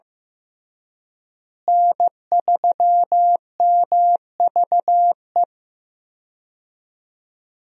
Morse code, heard as a keyed tone, about 15 wpm, 700 Hz. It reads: N3MVE
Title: USA Call Signs - 5 Characters - Repeated 3x - 15wpm